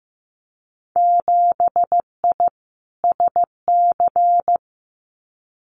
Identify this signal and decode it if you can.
Morse code, heard as a keyed tone, about 15 wpm, 700 Hz. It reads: 7I SC
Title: Winter Field Day Exchanges Encore 15wpm